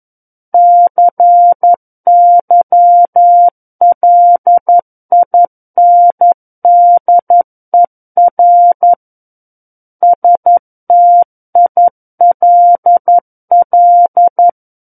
Morse code, heard as a keyed tone, about 11 wpm, 700 Hz. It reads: CYLINDER STILL
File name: 11 WPM